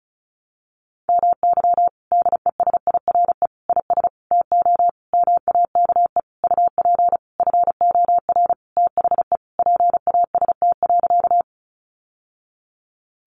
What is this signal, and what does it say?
Morse code, heard as a keyed tone, about 34 wpm, 700 Hz. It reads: MY DESIRE IS TO MAKE UP FOR THE PAST.
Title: Sentences from Top 1100 Words 35wpm